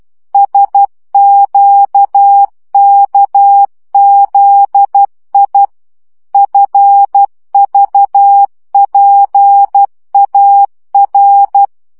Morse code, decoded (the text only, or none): SQKZI FVPAR